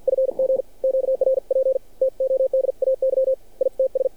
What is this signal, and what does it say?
Morse code, heard as a keyed tone, about 50 wpm, 530 Hz. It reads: FG QRP TODAY ITS
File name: archived / CRSnet